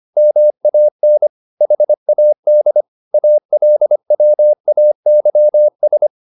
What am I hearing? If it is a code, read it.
Morse code, heard as a keyed tone, about 25 wpm, 600 Hz. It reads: MAN HAD ALWAYS
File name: the old man and the sea part 1 25wpm